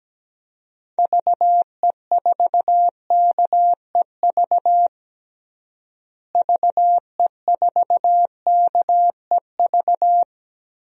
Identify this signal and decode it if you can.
Morse code, heard as a keyed tone, about 17 wpm, 700 Hz. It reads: VE4KEV VE4KEV
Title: DXCC-100 Call Signs - 6 Characters - Repeated 3x - 17wpm